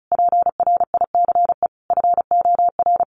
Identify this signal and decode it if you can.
Morse code, heard as a keyed tone, about 34 wpm, 700 Hz. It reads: PRICE FOR
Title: Sentences from Top 1100 Words 35wpm